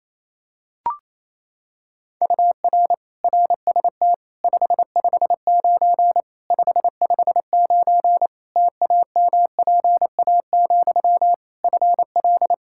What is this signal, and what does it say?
Morse code, heard as a keyed tone, about 28 wpm, 700 Hz. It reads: UR RST 559 559 TAMPA, FL
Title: Practice QSO - Exchange 1 - Beginning - 28wpm